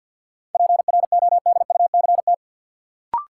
Morse code, heard as a keyed tone, about 50 wpm, 700 Hz. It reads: PRODUCT
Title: Sentences from Top 1400 Words - New Word Drill Alt - 50wpm